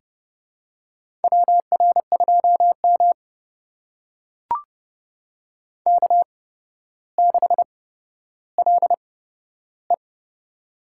Morse code, heard as a keyed tone, about 30 wpm, spaced as for 13 wpm, 700 Hz. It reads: WR2M K6LE
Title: USA Call Signs - 4 Characters - 8x Character Spacing - 30wpm